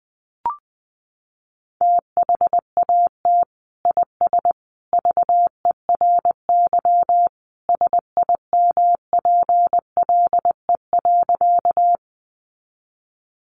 Morse code, heard as a keyed tone, about 19 wpm, 700 Hz. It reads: THAT IS VERY SIMPLE.
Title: Sentences from Top 400 Words 20wpm